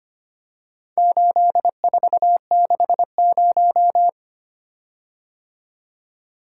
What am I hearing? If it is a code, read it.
Morse code, heard as a keyed tone, about 25 wpm, 700 Hz. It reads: 8460
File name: practice / 25wpm / Four Letter-Number Combinations - 25wpm